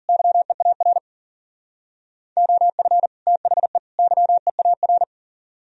Morse code, heard as a keyed tone, about 50 wpm, 700 Hz. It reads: YEAR OF THE YEAR